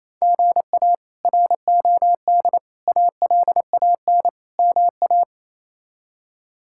Morse code, heard as a keyed tone, about 28 wpm, 700 Hz. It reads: GA ROD ALAN MA
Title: K1USN SST QSO Practice - 28wpm